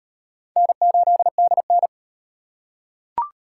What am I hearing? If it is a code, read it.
Morse code, heard as a keyed tone, about 37 wpm, 700 Hz. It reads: N8DN